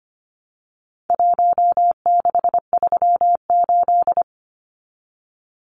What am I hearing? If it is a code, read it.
Morse code, heard as a keyed tone, about 25 wpm, 700 Hz. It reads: 1638
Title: Numbers - Contest Style - 4-Digit 25wpm